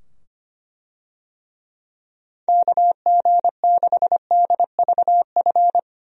Morse code, heard as a keyed tone, about 25 wpm, 700 Hz. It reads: KG6DVF
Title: USA Call Signs - 6 Characters - 25wpm